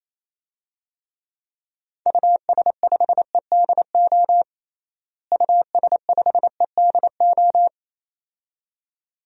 Morse code, heard as a keyed tone, about 28 wpm, 700 Hz. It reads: US5EDO US5EDO
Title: DXCC-100 Call Signs - 6 Characters - Repeated 2x - 28wpm